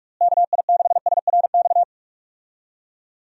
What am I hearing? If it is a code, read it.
Morse code, heard as a keyed tone, about 46 wpm, 700 Hz. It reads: KI6SRX
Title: USA Call Signs - 6 Characters - Repeated 2x - 45wpm